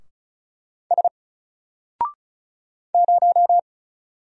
Morse code, heard as a keyed tone, about 36 wpm, 700 Hz. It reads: S 0
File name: Single-Letter-Number-35wpm